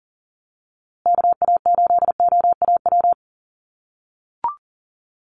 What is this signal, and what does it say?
Morse code, heard as a keyed tone, about 40 wpm, 700 Hz. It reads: KA8OAW
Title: USA Call Signs Repeated 3x 40wpm